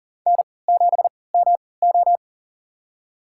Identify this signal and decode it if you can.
Morse code, heard as a keyed tone, about 41 wpm, spaced as for 23 wpm, 700 Hz. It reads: N7MO
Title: USA Call Signs - 3x Character Spacing - 40wpm